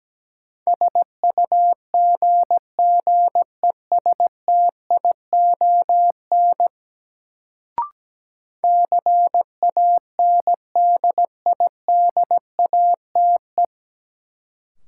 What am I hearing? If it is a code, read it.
Morse code, heard as a keyed tone, about 17 wpm, 700 Hz. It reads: SUGGESTION CANDIDATE
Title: Sentences from Top 1600 Words - New Word Drill - 17wpm